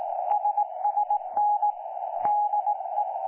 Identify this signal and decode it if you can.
Morse code, heard as a keyed tone, about 18 wpm, 830 Hz. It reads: SSN DE
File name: 09 38Z 18088.0kHz k3fef darrel